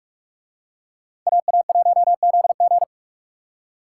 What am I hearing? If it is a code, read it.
Morse code, heard as a keyed tone, about 44 wpm, 700 Hz. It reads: AA1ZG